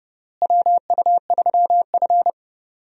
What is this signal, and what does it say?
Morse code, heard as a keyed tone, about 30 wpm, 700 Hz. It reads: WU3F
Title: USA Call Signs - 4 Characters - Repeated 3x - 30wpm